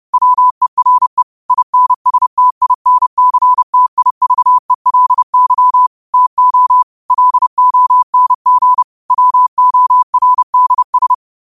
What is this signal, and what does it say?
Morse code, heard as a keyed tone, about 30 wpm, 1000 Hz. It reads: WERE INSTINCTIVELY TO LONG WORDS
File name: qotdfiles / 30wpm